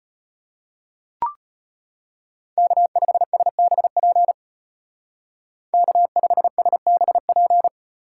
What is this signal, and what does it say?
Morse code, heard as a keyed tone, about 31 wpm, 700 Hz. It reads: K5SBP K5SBP
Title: Speed Racing - USA Call Signs 25wpm